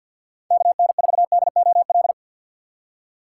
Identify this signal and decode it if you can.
Morse code, heard as a keyed tone, about 52 wpm, 700 Hz. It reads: KN4DOL
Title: USA Call Signs - 4x Character Spacing - 50wpm